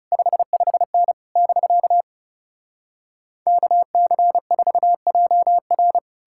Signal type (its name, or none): Morse code